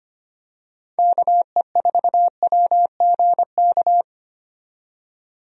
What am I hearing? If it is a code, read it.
Morse code, heard as a keyed tone, about 25 wpm, 700 Hz. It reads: KE4WGK
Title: USA Call Signs - 6 Characters - 25wpm